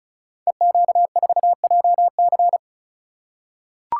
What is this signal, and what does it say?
Morse code, heard as a keyed tone, about 36 wpm, 700 Hz. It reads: EQ4JC
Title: Code Groups - 5 Letter:Numbers - 35wpm